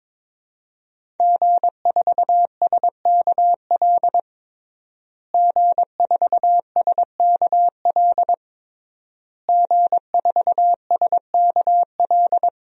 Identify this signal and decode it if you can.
Morse code, heard as a keyed tone, about 22 wpm, 700 Hz. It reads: G4SKL G4SKL G4SKL
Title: DXCC-100 Call Signs - 5 Characters - Repeated 3x - 22wpm